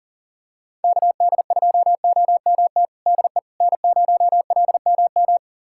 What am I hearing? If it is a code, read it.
Morse code, heard as a keyed tone, about 40 wpm, 700 Hz. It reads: KD2OMT DE N0LMM
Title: USA Callsign DE Callsign 40wpm